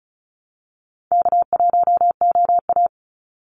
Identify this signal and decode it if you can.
Morse code, heard as a keyed tone, about 34 wpm, 700 Hz. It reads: K1OA
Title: Call Signs Repeated 2x - 35wpm